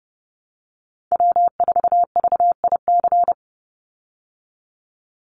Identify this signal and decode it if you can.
Morse code, heard as a keyed tone, about 30 wpm, 700 Hz. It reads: W4VIC
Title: Call Signs Repeated 2x - 30wpm